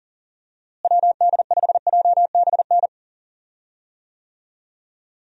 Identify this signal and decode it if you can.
Morse code, heard as a keyed tone, about 40 wpm, 700 Hz. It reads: WD5JBN
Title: USA Call Signs - 6 Characters - Repeated 2x - 40wpm